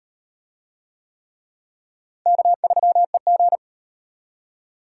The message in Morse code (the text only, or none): K3EG